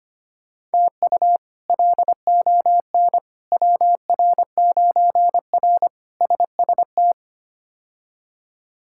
Morse code, heard as a keyed tone, about 25 wpm, 700 Hz. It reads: TU LON WR9R SST
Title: K1USN SST QSO Practice - 25wpm